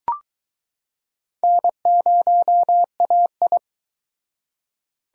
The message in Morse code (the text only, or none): N0AI